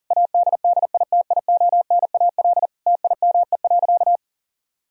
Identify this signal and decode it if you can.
Morse code, heard as a keyed tone, about 37 wpm, 700 Hz. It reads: ADDITIONAL TIME.
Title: Sentences from Top 1000 Words - Review Part 4 - 40wpm